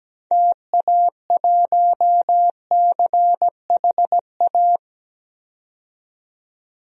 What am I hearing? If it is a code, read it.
Morse code, heard as a keyed tone, about 17 wpm, 700 Hz. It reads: TA1CHA
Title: DXCC-100 Call Signs - 6 Characters - 17wpm